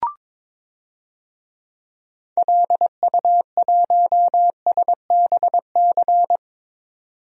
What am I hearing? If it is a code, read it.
Morse code, heard as a keyed tone, about 21 wpm, 700 Hz. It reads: LU1SBC
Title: DXCC-100 Call Signs - 6 Characters - 22wpm